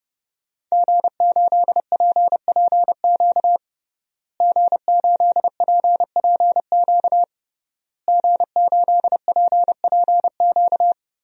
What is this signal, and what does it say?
Morse code, heard as a keyed tone, about 31 wpm, 700 Hz. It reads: G8PPQ G8PPQ G8PPQ
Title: DXCC-100 Call Signs - 5 Characters - Repeated 3x - 30wpm